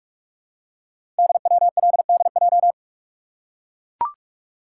Morse code, heard as a keyed tone, about 44 wpm, 700 Hz. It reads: DWLDJ
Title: Code Groups - 5 Letters - 45wpm